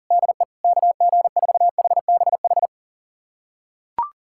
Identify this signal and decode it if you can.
Morse code, heard as a keyed tone, about 37 wpm, 700 Hz. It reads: DE KG4HBH
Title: USA Callsign DE Callsign 40wpm